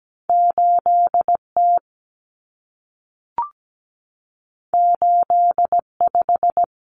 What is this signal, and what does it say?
Morse code, heard as a keyed tone, about 17 wpm, 700 Hz. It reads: 8T 85
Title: Common Cut Numbers - 2-Digit 17wpm